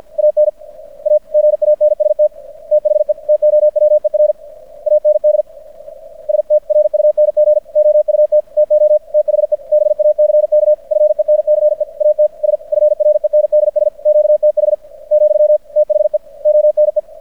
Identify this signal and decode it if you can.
Morse code, heard as a keyed tone, about 50 wpm, 600 Hz. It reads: AN A GANST THE TOWER AND ST RUNG OUT TO THE BACK FENCE AT S FRENDS QTH , THE ONE
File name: archived / CRSnet